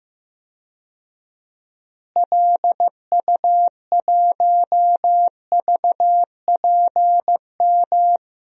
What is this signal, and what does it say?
Morse code, heard as a keyed tone, about 15 wpm, 700 Hz. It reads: LU1VPM
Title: DXCC-100 Call Signs - 6 Characters - Repeated 2x - 15wpm